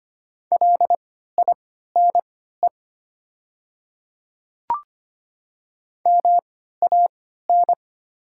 Morse code, heard as a keyed tone, about 25 wpm, spaced as for 14 wpm, 690 Hz. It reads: LINE MAN
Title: 3x Character Spacing - Top 200 Words - 25wpm